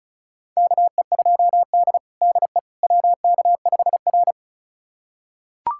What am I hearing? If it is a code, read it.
Morse code, heard as a keyed tone, about 36 wpm, 700 Hz. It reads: KE2D DE WK5R
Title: USA Callsign DE Callsign - 4 Characters - 35wpm